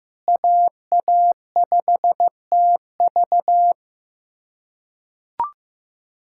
Morse code, heard as a keyed tone, about 15 wpm, 700 Hz. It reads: AA5TV
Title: USA Callsign DE Callsign 15wpm